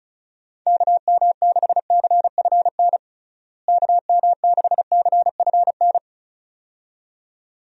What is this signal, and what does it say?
Morse code, heard as a keyed tone, about 36 wpm, 700 Hz. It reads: KM6CFN KM6CFN
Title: USA Call Signs - 6 Characters - Repeated 2x - 35wpm